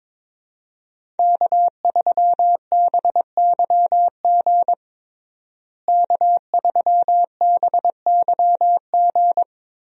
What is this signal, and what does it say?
Morse code, heard as a keyed tone, about 22 wpm, 700 Hz. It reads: K3BYG K3BYG
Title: USA Call Signs - 5 Characters - Repeated 3x - 22wpm